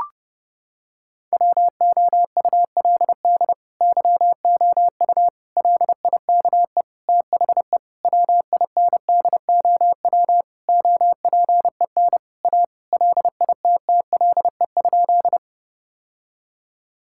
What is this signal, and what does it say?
Morse code, heard as a keyed tone, about 30 wpm, 700 Hz. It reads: WOULD YOU LIKE THE WINDOW OPEN A LITTLE?
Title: Sentences from Top 1600 Words 30wpm